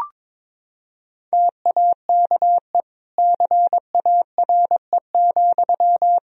Morse code, heard as a keyed tone, about 22 wpm, 700 Hz. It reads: TAKE CARE,
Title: Sentences from Top 1000 Words - Review Part 4 - 22wpm